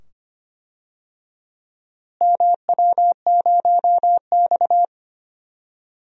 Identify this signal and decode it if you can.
Morse code, heard as a keyed tone, about 25 wpm, 700 Hz. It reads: MW0X
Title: DXCC-100 Call Signs - 4 Characters - Repeated 2x - 25wpm